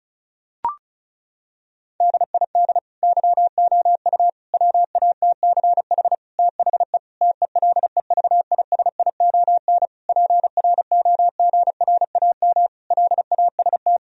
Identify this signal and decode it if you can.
Morse code, heard as a keyed tone, about 36 wpm, 700 Hz. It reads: DID YOU WATCH THE TELEVISION PROGRAM LAST
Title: Sentences from Top 1000 Words - Review Part 4 - 35wpm